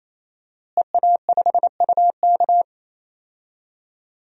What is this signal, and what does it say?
Morse code, heard as a keyed tone, about 28 wpm, 700 Hz. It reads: EA5UK
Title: DXCC-100 Call Signs - 5 Characters - Repeated 3x - 28wpm